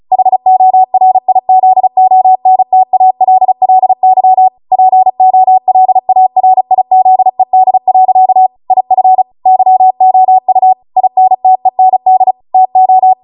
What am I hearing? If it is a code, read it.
Morse code, heard as a keyed tone, about 36 wpm, 750 Hz. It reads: HORIZONTALLY POLARIZED. IF YOU INTEND TO